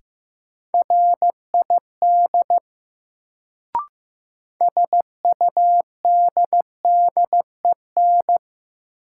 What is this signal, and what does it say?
Morse code, heard as a keyed tone, about 15 wpm, 700 Hz. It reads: RID SUDDEN
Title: Sets of 4 Words - New Word Drill - 15wpm